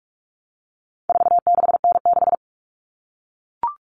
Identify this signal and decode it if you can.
Morse code, heard as a keyed tone, about 44 wpm, 700 Hz. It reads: 46N6
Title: Common Cut Numbers - 4-Digit 45wpm